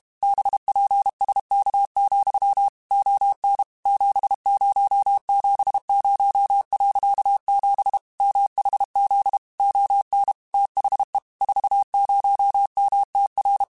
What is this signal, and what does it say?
Morse code, heard as a keyed tone, about 32 wpm, 800 Hz. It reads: BPSK, ON 7070.7 MHZ ON THE 40MTR
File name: QSOs / QSOs-mp3s / WA7QOX 2@30